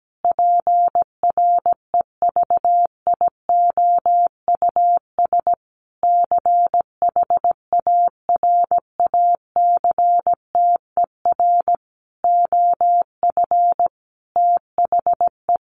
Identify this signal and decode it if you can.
Morse code, heard as a keyed tone, about 17 wpm, 700 Hz. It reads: PREVIOUS CHARACTER OF THE
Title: Sentences from Top 1500 Words 17wpm